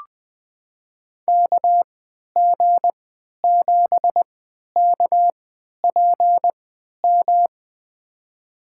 Morse code, heard as a keyed tone, about 21 wpm, spaced as for 11 wpm, 700 Hz. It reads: KG7KPM
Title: USA Call Signs - 6 Characters - 3x Character Spacing - 20wpm